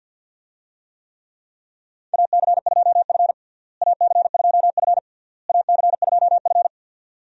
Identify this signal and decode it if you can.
Morse code, heard as a keyed tone, about 51 wpm, 700 Hz. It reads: AC2F AC2F AC2F
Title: USA Call Signs - 4 Characters - Repeated 3x - 50wpm